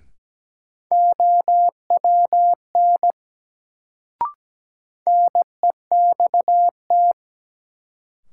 Morse code, heard as a keyed tone, about 17 wpm, 700 Hz. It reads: OWN NEXT
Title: Sentences from Top 300 Words - New Word Drill - 17wpm